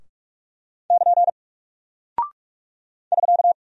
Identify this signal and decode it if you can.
Morse code, heard as a keyed tone, about 44 wpm, 700 Hz. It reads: <KN> <SK>
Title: Single Character (Letter:Number:Punctuation:Prosign) Alt 45wpm